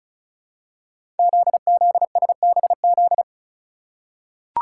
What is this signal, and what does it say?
Morse code, heard as a keyed tone, about 36 wpm, 700 Hz. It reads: ZZSBZ
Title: Code Groups - 5 Letters - 35wpm